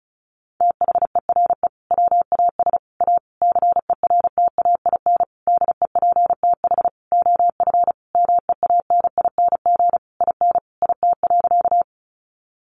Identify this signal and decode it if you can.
Morse code, heard as a keyed tone, about 34 wpm, 700 Hz. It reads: THERE WAS A CERTAIN DEPTH OF MEANING IN IT.
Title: Sentences from Top 1200 Words 35wpm